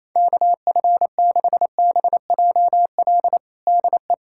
Morse code, heard as a keyed tone, about 28 wpm, 700 Hz. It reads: KF6BJL DE
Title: USA Callsign DE Callsign - 6 Characters - 28wpm